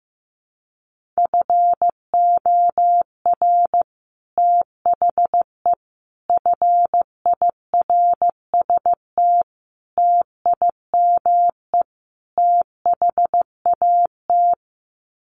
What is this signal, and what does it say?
Morse code, heard as a keyed tone, about 15 wpm, 700 Hz. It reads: FOR THE FIRST TIME THAT
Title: Sets of 5 Words Encore 15wpm